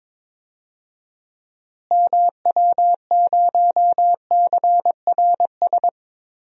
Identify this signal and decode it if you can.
Morse code, heard as a keyed tone, about 21 wpm, 700 Hz. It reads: MW0CRS
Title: DXCC-100 Call Signs - 6 Characters - Repeated 2x - 22wpm